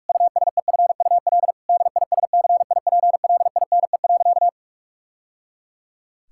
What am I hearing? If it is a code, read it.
Morse code, heard as a keyed tone, about 46 wpm, 700 Hz. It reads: USEFUL DISCIPLINE.
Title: Sentences from Top 1000 Words - Review Part 6 - 45wpm